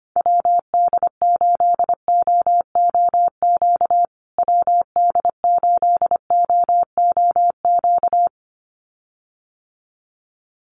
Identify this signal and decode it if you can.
Morse code, heard as a keyed tone, about 25 wpm, 700 Hz. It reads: WD8OOQ WD8OOQ
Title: USA Call Signs Repeated 2x 25wpm